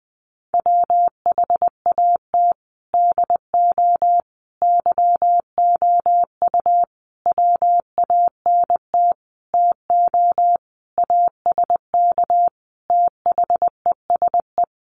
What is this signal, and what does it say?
Morse code, heard as a keyed tone, about 20 wpm, 700 Hz. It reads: WHAT DO YOU WANT TO ASK THESE
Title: Sentences from Top 200 Words 20wpm